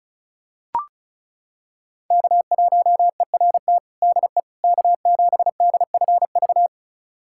Morse code, heard as a keyed tone, about 34 wpm, 700 Hz. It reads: K1ERT DE K7DFV
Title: USA Callsign DE Callsign - 5 Characters - 35wpm